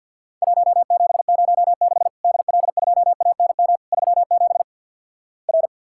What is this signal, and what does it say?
Morse code, heard as a keyed tone, about 48 wpm, 700 Hz. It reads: 1706 DL2ANM 37 R
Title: Worked All Europe DX Contest - QTC Exchange with Repeats - 50wpm